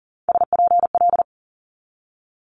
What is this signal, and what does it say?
Morse code, heard as a keyed tone, about 42 wpm, 700 Hz. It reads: SPL
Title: Multi-letter Phonemes 40wpm